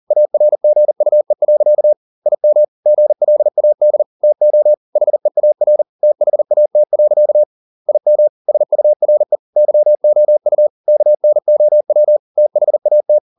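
Morse code, heard as a keyed tone, about 40 wpm, 600 Hz. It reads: ARGUE. IM GLAD TO HEAR THAT. IM SURE YOU KNOW THAT
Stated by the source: Morse code at 40 wpm, practice text, CW med Gnister.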